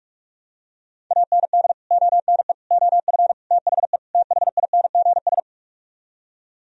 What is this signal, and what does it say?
Morse code, heard as a keyed tone, about 48 wpm, 700 Hz. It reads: AND ONE OF THE THINGS